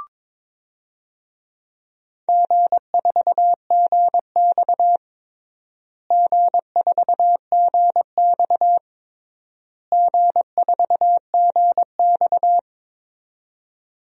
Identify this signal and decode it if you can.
Morse code, heard as a keyed tone, about 22 wpm, 700 Hz. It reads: G4GX G4GX G4GX
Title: DXCC-100 Call Signs - 4 Characters - Repeated 3x - 22wpm